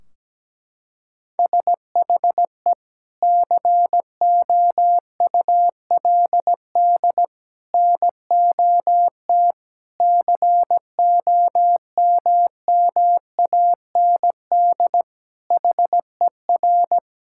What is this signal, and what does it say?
Morse code, heard as a keyed tone, about 17 wpm, 700 Hz. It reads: SHE COULD NOT COMMAND HER
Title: Sentences from Top 1600 Words 17wpm